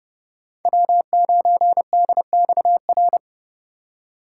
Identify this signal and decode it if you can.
Morse code, heard as a keyed tone, about 30 wpm, 700 Hz. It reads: W9DXR
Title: K1USN SST QSO Practice - 30wpm